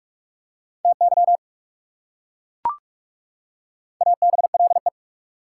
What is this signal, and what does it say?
Morse code, heard as a keyed tone, about 36 wpm, 690 Hz. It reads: TY ABLE
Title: Speed-Racing Prefixes and Suffixes 30wpm